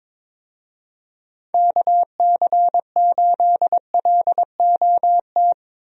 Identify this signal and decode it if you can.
Morse code, heard as a keyed tone, about 22 wpm, 700 Hz. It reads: KC8LOT
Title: USA Call Signs - 6 Characters - Repeated 2x - 22wpm